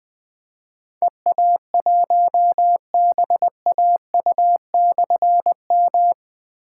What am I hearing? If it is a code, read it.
Morse code, heard as a keyed tone, about 20 wpm, 700 Hz. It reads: EA1BAU/M
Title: Difficult Call Signs 20wpm